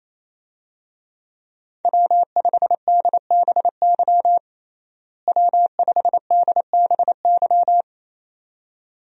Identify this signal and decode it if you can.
Morse code, heard as a keyed tone, about 28 wpm, 700 Hz. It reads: W5DBY W5DBY
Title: USA Call Signs - 5 Characters - Repeated 2x - 28wpm